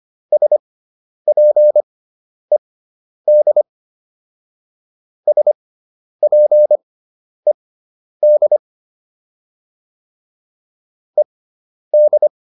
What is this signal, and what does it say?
Morse code, heard as a keyed tone, about 25 wpm, spaced as for 10 wpm, 600 Hz. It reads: SPED SPED ED